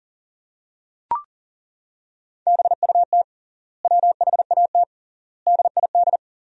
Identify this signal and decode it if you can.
Morse code, heard as a keyed tone, about 41 wpm, 700 Hz. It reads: BUT WHAT DID
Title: 3x Word-Spacing - Sentences from Top 200 Words - 40wpm